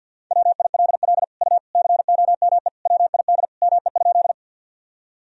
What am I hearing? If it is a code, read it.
Morse code, heard as a keyed tone, about 50 wpm, 700 Hz. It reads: WILL U COME WID ME?
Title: Sentences from Top 200 Words and Abbreviations 50wpm